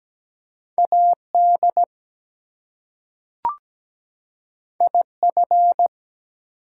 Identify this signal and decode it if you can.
Morse code, heard as a keyed tone, about 17 wpm, 700 Hz. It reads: AD IF
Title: Two Letter Words - 17wpm